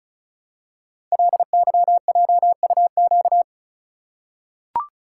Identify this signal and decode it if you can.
Morse code, heard as a keyed tone, about 35 wpm, 700 Hz. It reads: LYJUQ
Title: Code Groups - 5 Letters - 35wpm